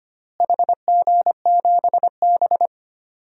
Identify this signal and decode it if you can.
Morse code, heard as a keyed tone, about 25 wpm, 700 Hz. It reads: HG7B
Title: DXCC-100 Call Signs - 4 Characters - Repeated 2x - 25wpm